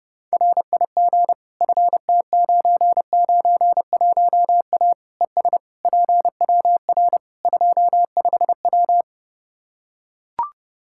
Morse code, heard as a keyed tone, about 30 wpm, 700 Hz. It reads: RIG FT991A ES PWR 25W
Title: Practice QSO - Exchange 2 - Info - 30wpm